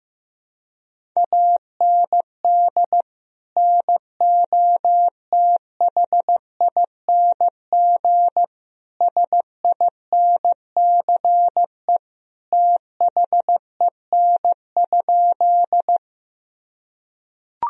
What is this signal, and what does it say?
Morse code, heard as a keyed tone, about 15 wpm, 700 Hz. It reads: AND NOTHING SINCE THEN?
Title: Sentences from Top 1000 Words - Review Part 1 - 15wpm